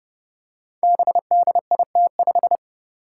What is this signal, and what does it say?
Morse code, heard as a keyed tone, about 28 wpm, 700 Hz. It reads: BDIT5
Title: Five Challenging Letter-Number Combinations 30wpm